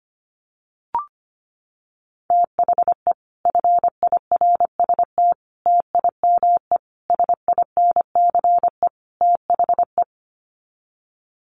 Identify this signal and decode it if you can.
Morse code, heard as a keyed tone, about 25 wpm, 700 Hz. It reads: THE FIRST TIME SINCE THE
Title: Sets of 5 Words Encore 25wpm